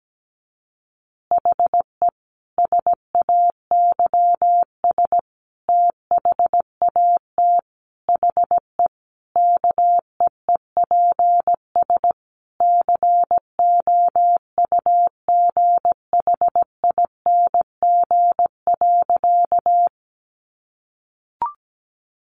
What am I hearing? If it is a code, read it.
Morse code, heard as a keyed tone, about 17 wpm, 700 Hz. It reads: HE SAYS THAT HE KEEPS COUGHING.
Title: Additional Easy Phrases 17wpm